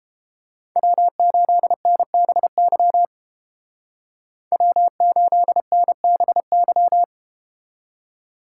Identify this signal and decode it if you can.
Morse code, heard as a keyed tone, about 32 wpm, 700 Hz. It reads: W8NBY W8NBY